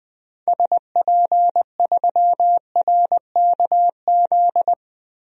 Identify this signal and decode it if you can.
Morse code, heard as a keyed tone, about 21 wpm, 700 Hz. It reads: SP3RKZ
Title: DXCC-100 Call Signs - 6 Characters - 20wpm